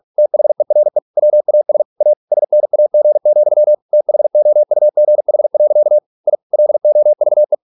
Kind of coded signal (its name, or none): Morse code